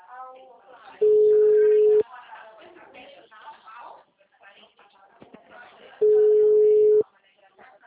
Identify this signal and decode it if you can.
Ringback tone (Europe, 425 Hz), with speech.
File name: out- -2032- - - .44628